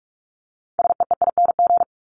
Morse code, heard as a keyed tone, about 44 wpm, 700 Hz. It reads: SEEING
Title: ING Suffix - Top 100 Words with Base Words 45wpm